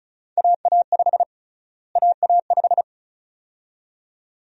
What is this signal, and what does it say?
Morse code, heard as a keyed tone, about 34 wpm, 700 Hz. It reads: AA5 AA5
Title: USA Call Sign Prefixes - Repeated 2x - 35wpm